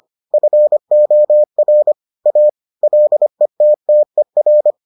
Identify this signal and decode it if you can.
Morse code, heard as a keyed tone, about 25 wpm, 600 Hz. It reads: FOR A LETTER